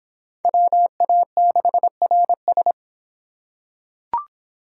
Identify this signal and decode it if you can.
Morse code, heard as a keyed tone, about 25 wpm, 700 Hz. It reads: WA6RS